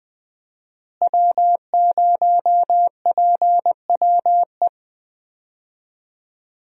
Morse code, heard as a keyed tone, about 21 wpm, 700 Hz. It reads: W0PWE
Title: USA Call Signs - 5 Characters - Repeated 3x - 20wpm